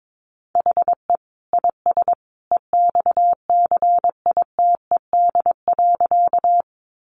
Morse code, heard as a keyed tone, about 22 wpm, 700 Hz. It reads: HE IS EXCITED.
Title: Additional Easy Phrases 22wpm